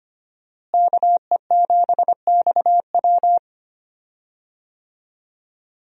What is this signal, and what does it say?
Morse code, heard as a keyed tone, about 25 wpm, 700 Hz. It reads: KE7XW
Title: USA Call Signs - 5 Characters - Repeated 2x - 25wpm